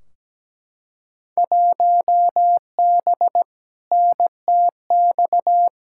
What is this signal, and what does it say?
Morse code, heard as a keyed tone, about 17 wpm, 700 Hz. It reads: 1B NTX
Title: Field Day Exchanges - 17wpm